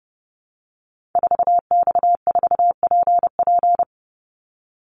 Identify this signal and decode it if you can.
Morse code, heard as a keyed tone, about 30 wpm, 700 Hz. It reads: 4X4PP
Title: Call Signs - 30wpm